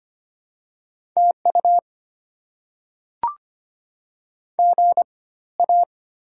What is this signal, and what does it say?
Morse code, heard as a keyed tone, about 25 wpm, spaced as for 16 wpm, 690 Hz. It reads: TU GA
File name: Top 90 QSO Elements - 4x Character Spacing - 25wpm